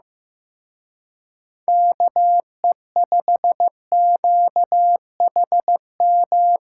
Morse code, heard as a keyed tone, about 18 wpm, 700 Hz. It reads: KE5QHM
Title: Speed Racing - USA 2x3 Call Signs 15wpm